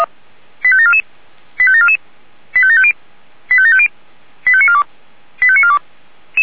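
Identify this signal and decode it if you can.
Five-tone selective call (ZVEI), heard as quick runs of five tones, tones of about 70 ms. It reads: 85744 85744 85750 85750 85942 85942
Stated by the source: ZVEI five-tone selective call 85730 received on an SDR.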